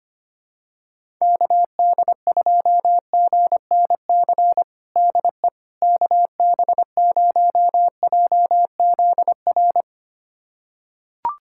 Morse code, heard as a keyed tone, about 25 wpm, 700 Hz. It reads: KD2GNC DE KB0JZR
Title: USA Callsign DE Callsign - 6 Characters - 25wpm